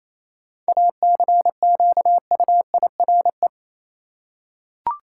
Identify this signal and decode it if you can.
Morse code, heard as a keyed tone, about 28 wpm, 700 Hz. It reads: ACQUIRE
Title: Seven Letter Words - 28wpm